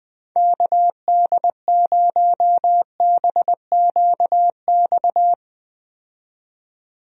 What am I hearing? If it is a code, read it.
Morse code, heard as a keyed tone, about 20 wpm, 700 Hz. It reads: KD0BQX
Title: USA Call Signs - 6 Characters - 20wpm